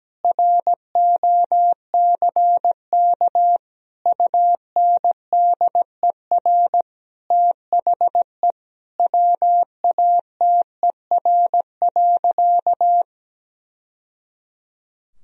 Morse code, heard as a keyed tone, about 17 wpm, 700 Hz. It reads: ROCK UNDER THE WATER.
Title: Sentences from Top 1000 Words - Review Part 2 - 17wpm